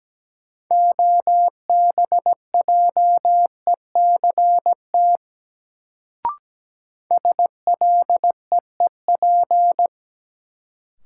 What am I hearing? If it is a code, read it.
Morse code, heard as a keyed tone, about 17 wpm, 700 Hz. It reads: OBJECT SLEEP
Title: Sentences from Top 500 Words - New Word Drill - 17wpm